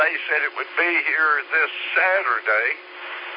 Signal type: Emergency Alert System (SAME) tones